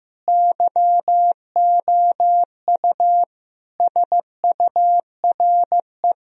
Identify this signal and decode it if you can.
Morse code, heard as a keyed tone, about 16 wpm, 700 Hz. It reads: YOU SURE
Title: Sentences from Top 1000 Words - Review Part 1 - 15wpm